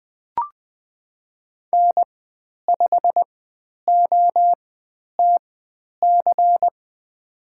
Morse code, heard as a keyed tone, about 20 wpm, spaced as for 10 wpm, 700 Hz. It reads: N5OTC
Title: 109 Lesson 18g - Letter m - Call Signs 20wpm